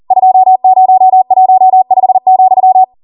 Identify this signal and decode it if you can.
Morse code, heard as a keyed tone, about 40 wpm, 750 Hz. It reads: 2015,